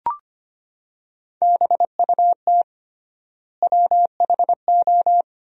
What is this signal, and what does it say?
Morse code, heard as a keyed tone, about 25 wpm, 700 Hz. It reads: BUT WHO
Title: 3x Word-Spacing - Sentences from Top 200 Words - 25wpm